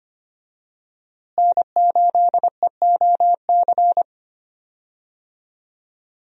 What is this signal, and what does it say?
Morse code, heard as a keyed tone, about 25 wpm, 700 Hz. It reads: N8EOC
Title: USA Call Signs - 5 Characters - Repeated 2x - 25wpm